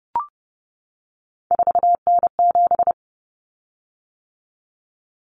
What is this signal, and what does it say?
Morse code, heard as a keyed tone, about 30 wpm, 700 Hz. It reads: 4N7
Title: Common Cut Numbers - 3-Digit 30wpm